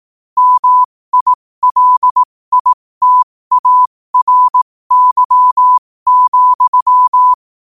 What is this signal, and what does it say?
Morse code, heard as a keyed tone, about 18 wpm, spaced as for 15 wpm, 1000 Hz. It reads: MILITARY,
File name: qotdfiles / 15wpm